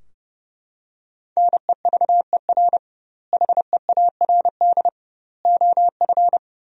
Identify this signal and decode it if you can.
Morse code, heard as a keyed tone, about 30 wpm, 700 Hz. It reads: NEVER HEARD OF
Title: 2x Word-Spacing - Sentences from Top 400 Words - 30wpm